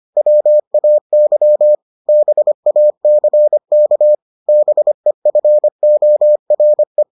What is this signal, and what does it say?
Morse code, heard as a keyed tone, about 25 wpm, 600 Hz. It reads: WAY BACK BEFORE